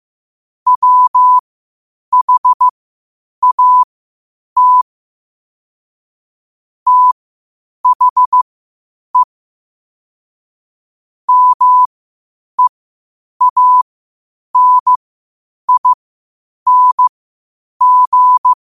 Morse code, heard as a keyed tone, about 15 wpm, spaced as for 8 wpm, 1000 Hz. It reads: WHAT THE MEANING
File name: qotdfiles / 8wpm